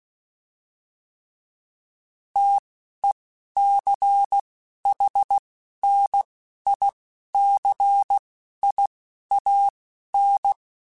Morse code, heard as a keyed tone, about 16 wpm, spaced as for 11 wpm, 800 Hz. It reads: TECHNICIAN
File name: fromSMs / FromSMs mp3s / Classes@10